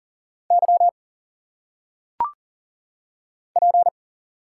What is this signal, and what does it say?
Morse code, heard as a keyed tone, about 40 wpm, 700 Hz. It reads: Y P
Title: Single Letters - Phonetic Pronunciation 40wpm